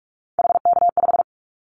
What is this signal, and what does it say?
Morse code, heard as a keyed tone, about 46 wpm, 700 Hz. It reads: HK5
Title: Callsign Prefixes for Top 100 DXCC - 45wpm